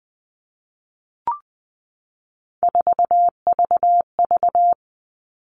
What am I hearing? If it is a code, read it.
Morse code, heard as a keyed tone, about 20 wpm, 700 Hz. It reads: 4VV
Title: All Cut Numbers - 3-Digit 20wpm